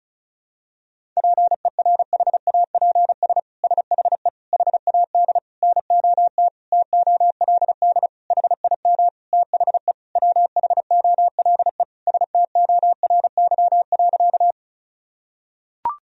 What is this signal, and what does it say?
Morse code, heard as a keyed tone, about 34 wpm, 700 Hz. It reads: PERHAPS SHE HAD NOT TOLD HIM THE WHOLE STORY.
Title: Sentences from Top 1000 Words - Review Part 1 - 35wpm